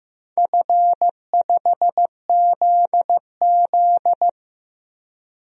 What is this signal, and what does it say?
Morse code, heard as a keyed tone, about 16 wpm, 700 Hz. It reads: F5ZZ
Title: DXCC-100 Call Signs - 4 Characters - Repeated 3x - 15wpm